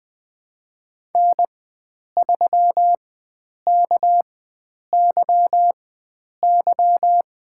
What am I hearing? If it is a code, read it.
Morse code, heard as a keyed tone, about 20 wpm, spaced as for 9 wpm, 700 Hz. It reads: N3KYY
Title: USA Call Signs - 5 Characters - 4x Character Spacing - 20wpm